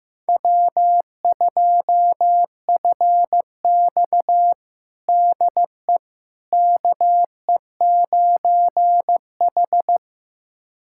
Morse code, heard as a keyed tone, about 16 wpm, 700 Hz. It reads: W2FX DE KE9H
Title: USA Callsign DE Callsign - 4 Characters - 15wpm